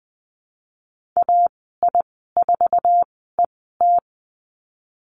Morse code, heard as a keyed tone, about 20 wpm, spaced as for 15 wpm, 700 Hz. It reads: AI4ET
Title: 048 Lesson 9g - Number 4 - Call Signs 20wpm